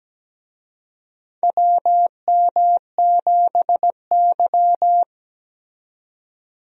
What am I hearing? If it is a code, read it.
Morse code, heard as a keyed tone, about 17 wpm, 700 Hz. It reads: WM7Y
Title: USA Call Signs - 4 Characters - 17wpm